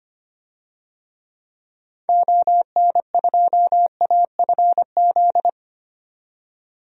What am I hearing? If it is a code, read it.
Morse code, heard as a keyed tone, about 25 wpm, 700 Hz. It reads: ON2AFZ
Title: DXCC-100 Call Signs - 6 Characters - 25wpm